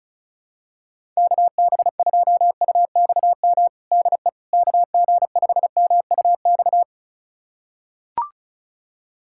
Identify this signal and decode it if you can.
Morse code, heard as a keyed tone, about 36 wpm, 700 Hz. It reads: KB2UXM DE KG5MUX
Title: USA Callsign DE Callsign - 6 Characters - 35wpm